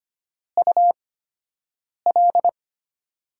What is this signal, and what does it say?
Morse code, heard as a keyed tone, about 25 wpm, spaced as for 7 wpm, 690 Hz. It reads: UL
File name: Top 90 QSO Elements - 8x Character Spacing - 25wpm